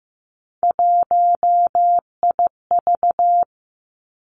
Morse code, heard as a keyed tone, about 15 wpm, 700 Hz. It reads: 1IV
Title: Three Challenging Letter-Number Combinations 15wpm